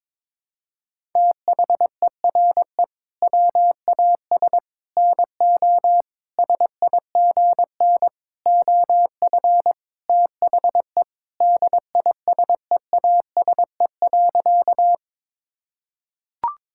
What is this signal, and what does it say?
Morse code, heard as a keyed tone, about 22 wpm, 700 Hz. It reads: THERE WAS NO SIGN OF THE DISEASE.
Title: Sentences from Top 1000 Words - Review Part 6 - 22wpm